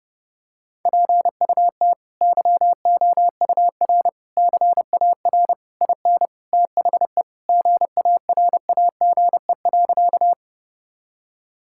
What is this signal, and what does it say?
Morse code, heard as a keyed tone, about 30 wpm, 700 Hz. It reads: PUT YOUR CAR IN THE GARAGE.
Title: Sentences from Top 1600 Words 30wpm